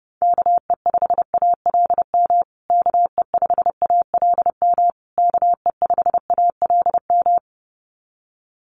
Morse code, heard as a keyed tone, about 30 wpm, 700 Hz. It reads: KE5ALM KE5ALM KE5ALM
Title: USA Call Signs Repeated 3x 30wpm